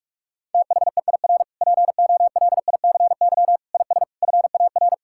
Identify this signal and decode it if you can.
Morse code, heard as a keyed tone, about 46 wpm, 690 Hz. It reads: THEIR POLICY IS FAR
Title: Sentences from Top 1000 Words - Review Part 5 - 45wpm